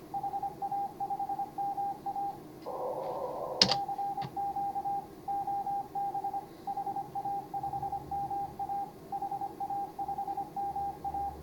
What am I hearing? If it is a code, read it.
Morse code, heard as a keyed tone, about 25 wpm, 800 Hz, with noise. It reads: HA5MA CQ OB HA5MA HA5MA
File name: CW / HA5MA RHOB HANG CW 2017